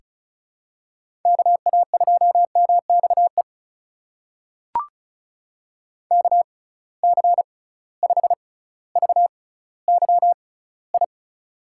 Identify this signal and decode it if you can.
Morse code, heard as a keyed tone, about 36 wpm, spaced as for 18 wpm, 700 Hz. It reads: KA2MXE KC5VYI
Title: USA Call Signs - 6 Characters - 6x Character Spacing - 35wpm